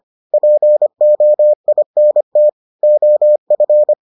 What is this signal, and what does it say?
Morse code, heard as a keyed tone, about 25 wpm, 600 Hz. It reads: POINT OF